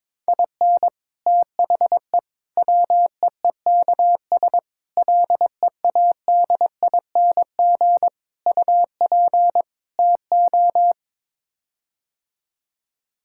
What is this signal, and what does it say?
Morse code, heard as a keyed tone, about 22 wpm, 700 Hz. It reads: IN THE WEEKS LEADING UP TO
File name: practice / 22wpm / Sets of 6 Words - 22wpm